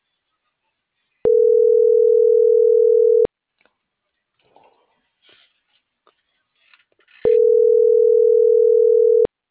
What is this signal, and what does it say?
Ringback tone (North America), 440 + 480 Hz.